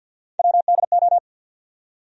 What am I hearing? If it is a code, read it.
Morse code, heard as a keyed tone, about 52 wpm, 700 Hz. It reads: WDO
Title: USA Call Sign Postfixes - 3 Letters 50wpm